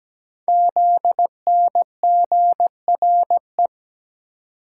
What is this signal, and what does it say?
Morse code, heard as a keyed tone, about 17 wpm, 700 Hz. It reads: ZNGRE
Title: Code Groups - 5 Letters - 17wpm